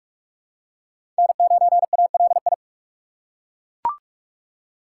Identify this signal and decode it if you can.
Morse code, heard as a keyed tone, about 41 wpm, 700 Hz. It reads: N9ALI
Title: USA Call Signs - 5 Characters - 2x Character Spacing - 45wpm